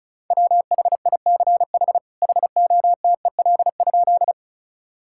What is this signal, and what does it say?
Morse code, heard as a keyed tone, about 36 wpm, 700 Hz. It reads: WHICH HOTEL?
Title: Sentences from Top 1600 Words 35wpm